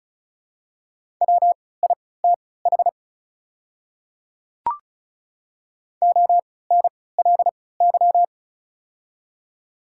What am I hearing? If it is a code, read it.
Morse code, heard as a keyed tone, about 36 wpm, spaced as for 20 wpm, 690 Hz. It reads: WITH ONLY
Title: 3x Character Spacing - Top 200 Words - 35wpm